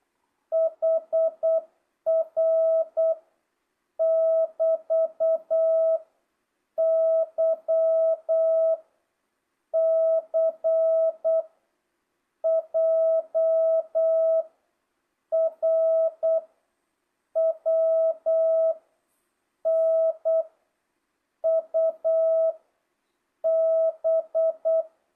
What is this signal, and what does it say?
Morse code, heard as a keyed tone, about 8 wpm, 660 Hz. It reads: HR <BT> Y C J R W N U B
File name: jh8cbh / 2025cw